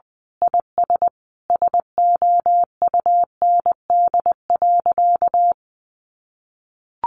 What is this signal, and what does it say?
Morse code, heard as a keyed tone, about 20 wpm, 700 Hz. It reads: IS SOUND.
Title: Sentences from Top 1400 Words 20wpm